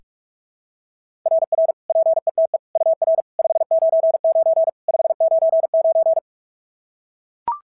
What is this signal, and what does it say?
Morse code, heard as a keyed tone, about 44 wpm, 690 Hz. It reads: RR PETE UR 599 599
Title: SOTA QSO Practice 45wpm